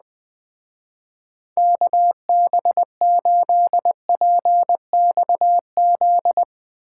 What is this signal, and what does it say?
Morse code, heard as a keyed tone, about 20 wpm, 700 Hz. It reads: KB8PXZ
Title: Speed Racing - USA 2x3 Call Signs 17wpm